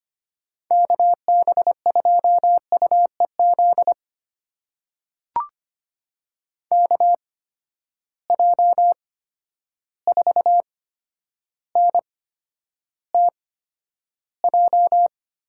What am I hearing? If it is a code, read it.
Morse code, heard as a keyed tone, about 25 wpm, spaced as for 11 wpm, 700 Hz. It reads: KB2UEZ KJ4NTJ
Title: USA Call Signs - 6 Characters - 8x Character Spacing - 25wpm